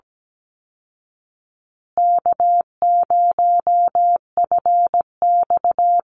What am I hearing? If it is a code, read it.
Morse code, heard as a keyed tone, about 17 wpm, 700 Hz. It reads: K0FX
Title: Easy Call Signs Repeated 3x - 17wpm